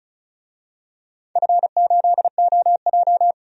Morse code, heard as a keyed tone, about 33 wpm, 690 Hz. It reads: F8OJ
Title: DXCC-100 Call Signs - 4 Characters - 35wpm